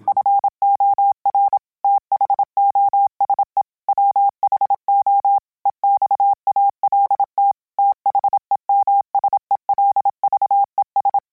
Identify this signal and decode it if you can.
Morse code, heard as a keyed tone, about 27 wpm, 800 Hz. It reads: FOR THOSE WHO EXALT THEMSELVES